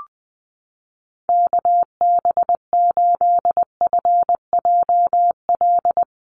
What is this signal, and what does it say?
Morse code, heard as a keyed tone, about 20 wpm, 700 Hz. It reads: KB8FJL
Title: USA Call Signs Repeated 3x 20wpm